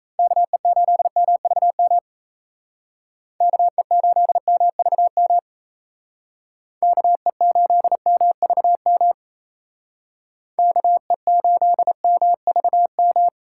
Morse code, heard as a keyed tone, about 36 wpm, 700 Hz. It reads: KE8MVM KE8MVM KE8MVM KE8MVM